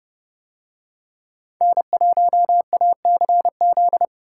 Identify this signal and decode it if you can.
Morse code, heard as a keyed tone, about 29 wpm, 700 Hz. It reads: N1ACZ